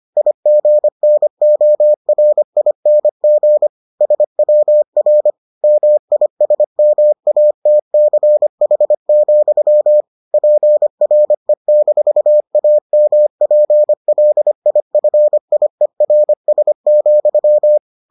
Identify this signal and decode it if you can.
Morse code, heard as a keyed tone, about 25 wpm, 600 Hz. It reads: IGNORING SWR MISMATCH, PRE-AMPLIFIERS,